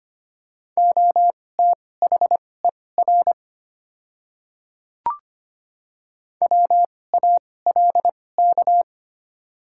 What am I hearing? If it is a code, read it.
Morse code, heard as a keyed tone, about 25 wpm, spaced as for 18 wpm, 700 Hz. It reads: OTHER WALK
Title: 2x Character Spacing - Top 500 Words - 25wpm